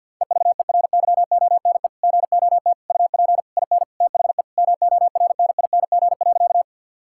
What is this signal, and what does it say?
Morse code, heard as a keyed tone, about 50 wpm, 700 Hz. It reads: EVERYONE GOT UP IN THE MORNING.
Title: Sentences from Top 1000 Words - Review Part 5 - 50wpm